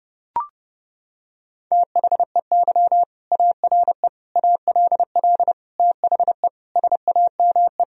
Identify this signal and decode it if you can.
Morse code, heard as a keyed tone, about 32 wpm, 700 Hz. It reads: THEY ARE ALL THE SAME
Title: Sentences from Top 1000 Words - Review Part 4 - 30wpm